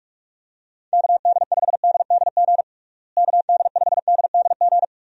Morse code, heard as a keyed tone, about 46 wpm, 700 Hz. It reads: KD5DDG KD5DDG
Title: USA Call Signs - 6 Characters - Repeated 2x - 45wpm